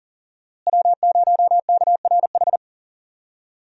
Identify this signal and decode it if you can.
Morse code, heard as a keyed tone, about 40 wpm, 700 Hz. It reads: W0KRH
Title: USA Call Signs - 5 Characters - Repeated 2x - 40wpm